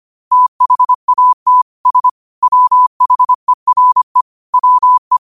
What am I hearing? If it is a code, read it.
Morse code, heard as a keyed tone, about 25 wpm, 1000 Hz. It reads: THAT S WHERE WE
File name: qotdfiles / 25wpm